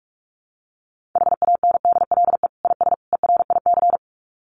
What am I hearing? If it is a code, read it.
Morse code, heard as a keyed tone, about 44 wpm, 700 Hz. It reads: HANDLE IS ERIC
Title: Common QSO Phrases 45wpm